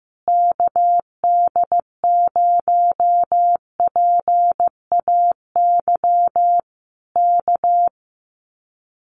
Morse code, heard as a keyed tone, about 15 wpm, 700 Hz. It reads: KD0PAY K
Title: Common QSO Phrases Encore 15wpm